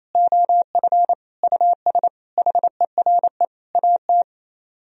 Morse code, heard as a keyed tone, about 38 wpm, spaced as for 33 wpm, 700 Hz. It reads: OF US HERE AT
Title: Speed-Racing - Sets of 5 Words 28wpm